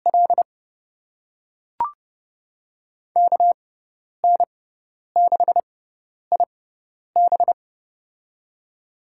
Morse code, heard as a keyed tone, about 30 wpm, spaced as for 10 wpm, 700 Hz. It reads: L KN6IB
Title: USA Call Signs - 5 Characters - 6x Character Spacing - 30wpm